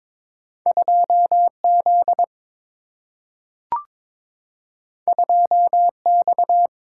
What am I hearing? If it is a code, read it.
Morse code, heard as a keyed tone, about 21 wpm, 700 Hz. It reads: 2Z 2X
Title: Single-Letter Single-Number 22wpm